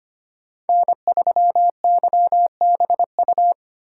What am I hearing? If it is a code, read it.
Morse code, heard as a keyed tone, about 25 wpm, 700 Hz. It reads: N3YBU
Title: USA Call Signs - 5 Characters - 3x Character Spacing - 25wpm